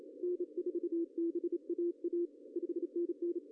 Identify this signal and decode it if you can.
Morse code, heard as a keyed tone, about 28 wpm, 400 Hz. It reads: N4BAA 5NN
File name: CW / cqww-cw-k5tr